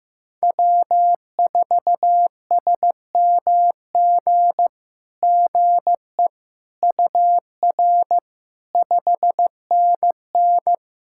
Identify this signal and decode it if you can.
Morse code, heard as a keyed tone, about 15 wpm, 700 Hz. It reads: W4SMG GE UR 5NN
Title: SOTA QSO Practice 15wpm